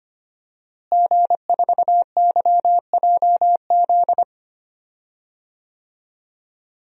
Morse code, heard as a keyed tone, about 25 wpm, 700 Hz. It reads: G4YJZ
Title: DXCC-100 Call Signs - 5 Characters - Repeated 3x - 25wpm